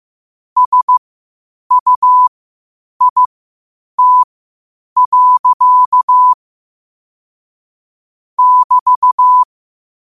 Morse code, heard as a keyed tone, about 15 wpm, spaced as for 8 wpm, 1000 Hz. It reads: SUIT. <BT>
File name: qotdfiles / 8wpm